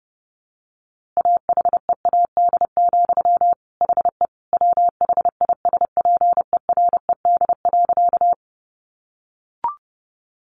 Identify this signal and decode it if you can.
Morse code, heard as a keyed tone, about 30 wpm, 700 Hz. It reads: AHEAD, HE WHISPERED.
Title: A Sound Of Thunder 30wpm